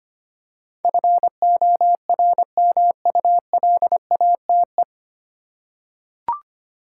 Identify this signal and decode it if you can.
Morse code, heard as a keyed tone, about 25 wpm, 690 Hz. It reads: FORMULATE
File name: practice / 25wpm / 9 Letter Words - First 3 Letters Given - 25wpm